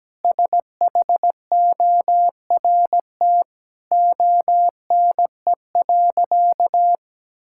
Morse code, heard as a keyed tone, about 17 wpm, 700 Hz. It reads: SHORT ONE.
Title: Sentences from Top 1600 Words 17wpm